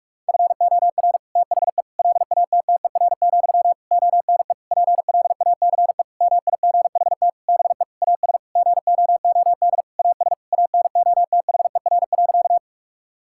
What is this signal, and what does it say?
Morse code, heard as a keyed tone, about 46 wpm, 700 Hz. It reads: FOR THE LATTER, ONE PLACE MIGHT BE AS GOOD AS ANOTHER.
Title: Sentences from Top 1600 Words 45wpm